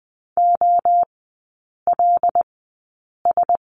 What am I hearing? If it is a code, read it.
Morse code, heard as a keyed tone, about 20 wpm, spaced as for 8 wpm, 700 Hz. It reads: OLS
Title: 074 Lesson 13e - Letter L - Three Characters 20wpm